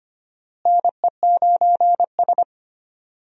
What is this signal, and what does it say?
Morse code, heard as a keyed tone, about 25 wpm, 700 Hz. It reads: NE9S
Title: USA Call Signs - 4 Characters - Repeated 2x - 25wpm